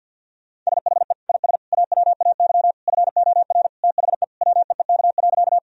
Morse code, heard as a keyed tone, about 50 wpm, 700 Hz. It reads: SHE IS AWAY FOR THE WEEK.
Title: Sentences from Top 1000 Words - Review Part 2 - 50wpm